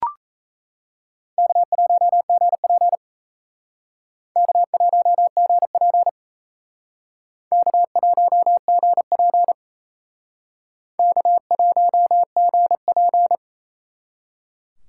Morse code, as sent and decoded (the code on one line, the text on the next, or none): -.- .---- --. .--. / -.- .---- --. .--. / -.- .---- --. .--. / -.- .---- --. .--.
K1GP K1GP K1GP K1GP